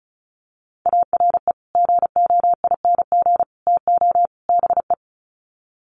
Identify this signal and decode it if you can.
Morse code, heard as a keyed tone, about 34 wpm, 700 Hz. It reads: ARE GOING TO BE
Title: Sets of 4 Words 35wpm